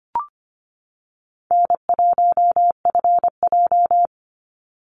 Morse code, heard as a keyed tone, about 25 wpm, 700 Hz. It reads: N1FJ
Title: NA SOTA Activator Callsigns 25wpm